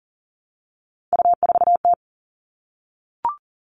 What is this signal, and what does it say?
Morse code, heard as a keyed tone, about 40 wpm, 700 Hz. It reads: U4T
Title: All Cut Numbers - 3-Digit 40wpm